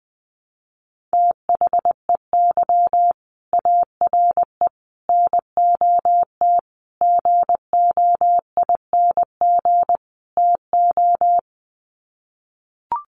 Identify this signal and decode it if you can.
Morse code, heard as a keyed tone, about 20 wpm, 700 Hz. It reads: THEY ARE NOT GOING TO
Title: Sets of 5 Words Encore2 20wpm